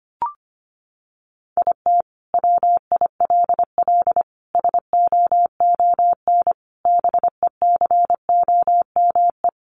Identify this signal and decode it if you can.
Morse code, heard as a keyed tone, about 25 wpm, 700 Hz. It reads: IT WILL SOON BECOME
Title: Sentences from Top 1000 Words 25wpm